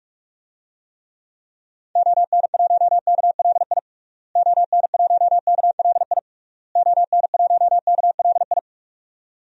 Morse code, heard as a keyed tone, about 46 wpm, 700 Hz. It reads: ON1KLI ON1KLI ON1KLI
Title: DXCC-100 Call Signs - 6 Characters - Repeated 3x - 45wpm